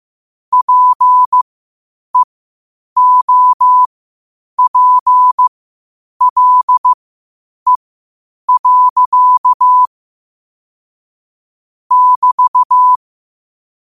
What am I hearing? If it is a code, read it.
Morse code, heard as a keyed tone, about 15 wpm, spaced as for 8 wpm, 1000 Hz. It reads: PEOPLE. <BT>
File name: qotdfiles / 8wpm